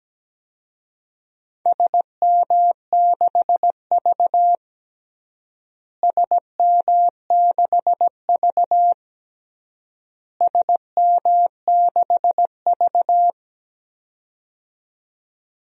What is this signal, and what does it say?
Morse code, heard as a keyed tone, about 17 wpm, 700 Hz. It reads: SM6V SM6V SM6V
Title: DXCC-100 Call Signs - 4 Characters - Repeated 3x - 17wpm